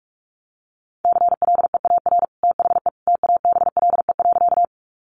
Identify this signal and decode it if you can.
Morse code, heard as a keyed tone, about 44 wpm, 700 Hz. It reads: CLEAR THE TABLE.
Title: Sentences from Top 500 Words 45wpm